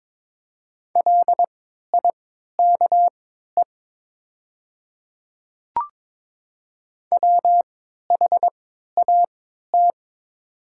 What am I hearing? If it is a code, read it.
Morse code, heard as a keyed tone, about 21 wpm, spaced as for 12 wpm, 700 Hz. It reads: LIKE WHAT
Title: 3x Character Spacing - Top 200 Words - 22wpm